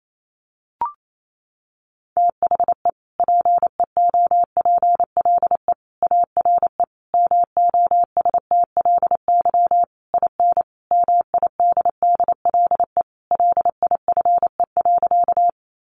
Morse code, heard as a keyed tone, about 28 wpm, 700 Hz. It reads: THE PEOPLE ARE MOSTLY IN MIDDLE LIFE.
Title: Sentences from Top 1300 Words 28wpm